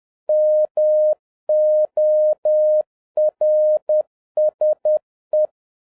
Morse code, heard as a keyed tone, about 10 wpm, 610 Hz. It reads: MORSE